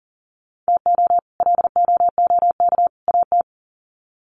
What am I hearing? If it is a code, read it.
Morse code, heard as a keyed tone, about 37 wpm, 700 Hz. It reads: TO LOOK AT
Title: Sets of 3 Words 40wpm